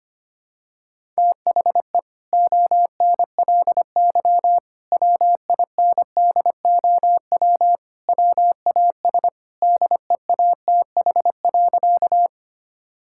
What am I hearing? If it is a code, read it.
Morse code, heard as a keyed tone, about 25 wpm, 700 Hz. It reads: THE ONLY WINDOW WAS DEATH.
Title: Sentences from Top 1600 Words 25wpm